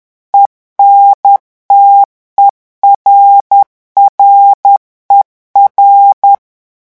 Morse code, heard as a keyed tone, about 10 wpm, 800 Hz. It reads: ENTERRER
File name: morse mp3 / revision francais 10 mots minute